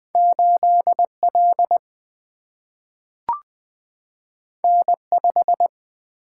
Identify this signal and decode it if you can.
Morse code, heard as a keyed tone, about 20 wpm, 700 Hz. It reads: 8 L N 5
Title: Single-Letter Single-Number 20wpm